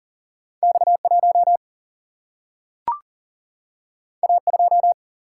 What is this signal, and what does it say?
Morse code, heard as a keyed tone, about 39 wpm, 700 Hz. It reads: X1 A2
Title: Single-Letter Single-Number 40wpm